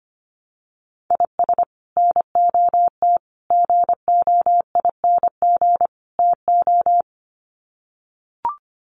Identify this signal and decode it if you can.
Morse code, heard as a keyed tone, about 25 wpm, 700 Hz. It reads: IS NOT GOING TO
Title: Sets of 4 Words 25wpm